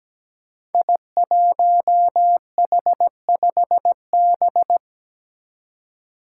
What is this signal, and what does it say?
Morse code, heard as a keyed tone, about 17 wpm, 700 Hz. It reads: I1H5B
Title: Five Challenging Letter-Number Combinations 17wpm